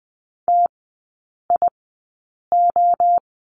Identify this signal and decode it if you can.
Morse code, heard as a keyed tone, about 20 wpm, spaced as for 8 wpm, 700 Hz. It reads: TIO
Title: 053 Lesson 10e - Letter r - Three Characters 20wpm